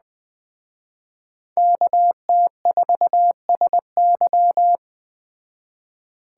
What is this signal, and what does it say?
Morse code, heard as a keyed tone, about 20 wpm, 700 Hz. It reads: KT4SY
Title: Speed Racing - USA 2x2 Call Signs 17wpm